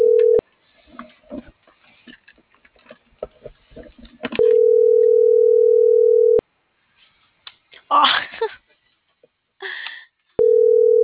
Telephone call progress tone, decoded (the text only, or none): ringback tone